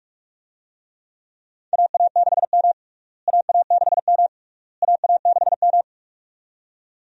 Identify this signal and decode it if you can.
Morse code, heard as a keyed tone, about 46 wpm, 700 Hz. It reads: AA6M AA6M AA6M
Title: USA Call Signs - 4 Characters - Repeated 3x - 45wpm